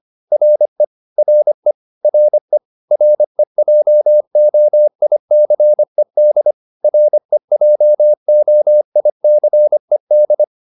Morse code, heard as a keyed tone, about 25 wpm, 600 Hz. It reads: RE RE RE REJOICED REJOICED